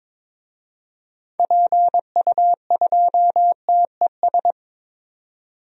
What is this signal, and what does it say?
Morse code, heard as a keyed tone, about 22 wpm, 700 Hz. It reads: PU2TES
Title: DXCC-100 Call Signs - 6 Characters - 22wpm